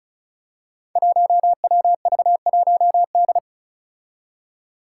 Morse code, heard as a keyed tone, about 36 wpm, 700 Hz. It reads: 1WV1D
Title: Five Challenging Letter-Number Combinations 35wpm